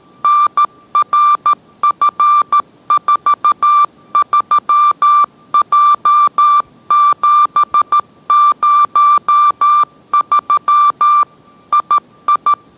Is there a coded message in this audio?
Yes: Morse code from 0.2 to 12.5 s, heard as a keyed tone, about 13 wpm, 1180 Hz. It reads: NRF43J703 II